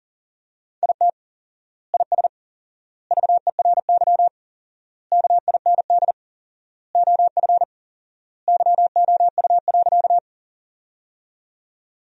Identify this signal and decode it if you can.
Morse code, heard as a keyed tone, about 40 wpm, 700 Hz. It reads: IT IS VERY KIND OF YOU.
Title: 4x Word-Spacing - Sentences from Top 200 Words - 40wpm